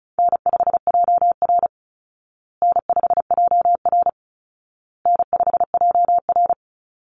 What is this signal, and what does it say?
Morse code, heard as a keyed tone, about 34 wpm, 700 Hz. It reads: N5JR N5JR N5JR
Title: Call Signs Repeated 2x - 35wpm